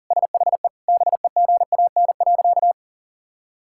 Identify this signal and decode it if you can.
Morse code, heard as a keyed tone, about 40 wpm, 700 Hz. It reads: SHE BEGAN.
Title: Sentences from Top 1000 Words - Review Part 5 - 40wpm